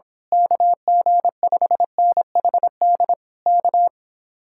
Morse code, heard as a keyed tone, about 26 wpm, 700 Hz. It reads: KG5NHD K
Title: Speed-Racing Common QSO Phrases - 22wpm